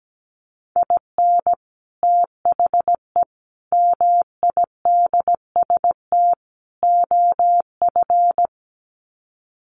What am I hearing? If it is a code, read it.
Morse code, heard as a keyed tone, about 17 wpm, 700 Hz. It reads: IN THE MIDST OF
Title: Sets of 4 Words 17wpm